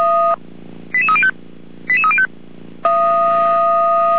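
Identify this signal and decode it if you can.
Five-tone selective call (ZVEI), heard as quick runs of five tones, tones of about 70 ms. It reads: 88305 88305
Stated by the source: ZVEI five-tone selective call received on an SDR.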